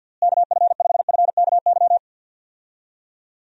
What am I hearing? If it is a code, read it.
Morse code, heard as a keyed tone, about 51 wpm, 700 Hz. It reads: KF5FKY
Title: USA Call Signs - 6 Characters - Repeated 2x - 50wpm